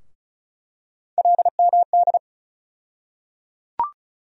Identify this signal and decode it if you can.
Morse code, heard as a keyed tone, about 36 wpm, 700 Hz. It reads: LMD
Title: USA Call Sign Postfixes - 3 Letters 35wpm